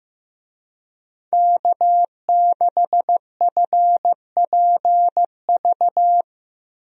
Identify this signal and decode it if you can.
Morse code, heard as a keyed tone, about 15 wpm, 700 Hz. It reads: K6FPV
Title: USA Call Signs - 5 Characters - Repeated 3x - 15wpm